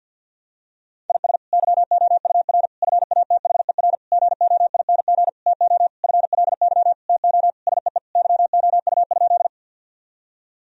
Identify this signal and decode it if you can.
Morse code, heard as a keyed tone, about 47 wpm, 700 Hz. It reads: IS YOUR FATHER GOING TO FLY TO SEE YOU?
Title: Sentences from Top 1000 Words - Review Part 2 - 50wpm